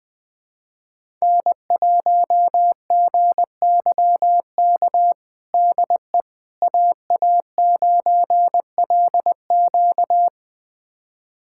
Morse code, heard as a keyed tone, about 20 wpm, 700 Hz. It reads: N1GYK DE AA9LQ
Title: USA Callsign DE Callsign - 5 Characters - 20wpm